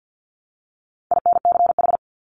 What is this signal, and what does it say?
Morse code, heard as a keyed tone, about 50 wpm, 700 Hz. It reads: INCH
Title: Top 500 Words Repeated 50wpm